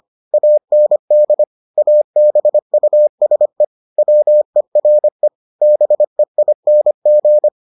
Morse code, heard as a keyed tone, about 25 wpm, 600 Hz. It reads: AND ABUSE WERE BEING